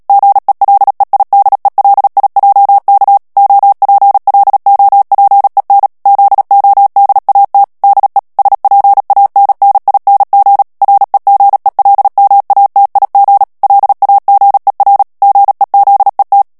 Morse code, heard as a keyed tone, about 37 wpm, 800 Hz. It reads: GELEIDELIJK OPLOPEN ZODAT DE SPANNING REGELMATIG LAGER GEZET
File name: morse mp3 / HOLLANDAIS texte clair 2 vitesse 35 mots minute